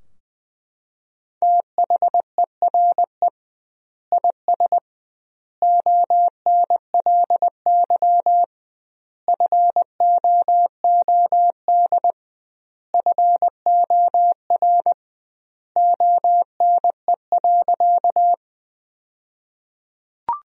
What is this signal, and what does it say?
Morse code, heard as a keyed tone, about 21 wpm, 700 Hz. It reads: THERE IS ONLY FOOD FOR ONE.
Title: 2x Word-Spacing - Sentences from Top 300 Words - 20wpm